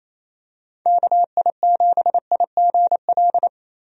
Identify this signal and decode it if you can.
Morse code, heard as a keyed tone, about 28 wpm, 700 Hz. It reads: KI7IGL
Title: USA Call Signs - 6 Characters - 28wpm